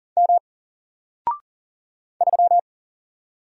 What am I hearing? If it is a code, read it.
Morse code, heard as a keyed tone, about 40 wpm, 700 Hz. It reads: M 3
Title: Single Letter-Number Alt 40wpm